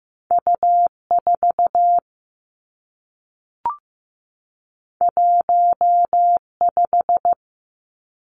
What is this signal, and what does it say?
Morse code, heard as a keyed tone, about 15 wpm, 700 Hz. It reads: U 4 1 5
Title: Two Challenging Letter-Number Combinations 15wpm